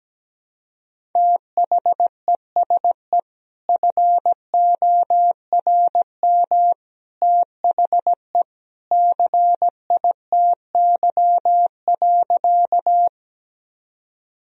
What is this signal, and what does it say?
Morse code, heard as a keyed tone, about 17 wpm, 700 Hz. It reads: THESE FORM THE CITY.
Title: Sentences from Top 1000 Words - Review Part 1 - 17wpm